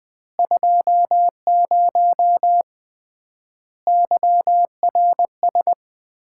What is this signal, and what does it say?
Morse code, heard as a keyed tone, about 20 wpm, 700 Hz. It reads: 20 YRS
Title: Common QSO Phrases - 3x Word-Spacing - 20wpm